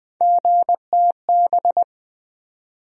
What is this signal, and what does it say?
Morse code, heard as a keyed tone, about 20 wpm, 700 Hz. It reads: GTB
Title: USA Call Sign Postfixes - 3 Letters 20wpm